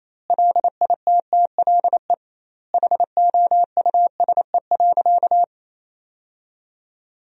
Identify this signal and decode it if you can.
Morse code, heard as a keyed tone, about 28 wpm, 700 Hz. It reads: LITTLE HOUSE.
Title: 2x Word-Spacing - Sentences from Top 200 Words - 28wpm